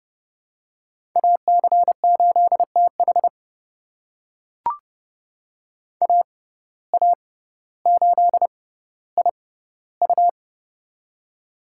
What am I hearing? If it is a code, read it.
Morse code, heard as a keyed tone, about 30 wpm, spaced as for 15 wpm, 700 Hz. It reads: AC8TH AA8IU
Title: USA Call Signs - 5 Characters - 6x Character Spacing - 30wpm